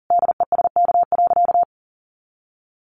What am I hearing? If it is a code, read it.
Morse code, heard as a keyed tone, about 40 wpm, 700 Hz. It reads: DESK.
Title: A Sound Of Thunder 40wpm